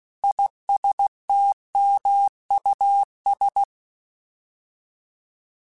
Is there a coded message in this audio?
Yes: Morse code from 0.2 to 3.6 s, heard as a keyed tone, about 16 wpm, 800 Hz. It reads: ISTMUS